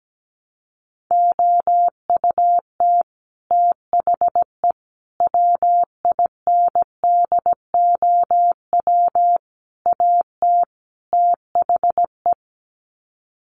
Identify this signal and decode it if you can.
Morse code, heard as a keyed tone, about 17 wpm, 700 Hz. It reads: OUT THE WINDOW AT THE
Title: Sets of 5 Words Encore2 17wpm